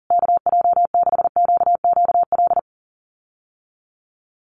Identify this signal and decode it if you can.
Morse code, heard as a keyed tone, about 40 wpm, 700 Hz. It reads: KJ6QQL
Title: USA Call Signs Repeated 3x 40wpm